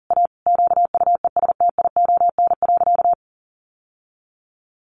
Morse code, heard as a keyed tone, about 40 wpm, 700 Hz. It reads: A QUESTION.
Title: Sentences from Top 600 Words 40wpm